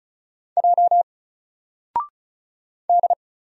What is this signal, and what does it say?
Morse code, heard as a keyed tone, about 34 wpm, 700 Hz. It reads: J D
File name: Single-Letter-Number-35wpm